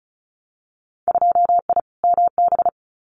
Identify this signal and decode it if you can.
Morse code, heard as a keyed tone, about 36 wpm, 700 Hz. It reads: 2I MB
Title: Winter Field Day Exchanges Encore 35wpm